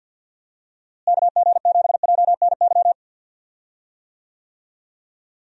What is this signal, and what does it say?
Morse code, heard as a keyed tone, about 51 wpm, 700 Hz. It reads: KG7JNY
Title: USA Call Signs - 6 Characters - Repeated 2x - 50wpm